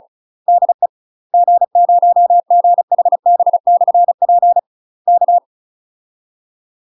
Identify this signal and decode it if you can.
Morse code, heard as a keyed tone, about 34 wpm, 700 Hz. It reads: DE G0GHB/P K
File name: UK MASTER QSOs TRAINING / 35WPM / UK MASTER QSOs 0033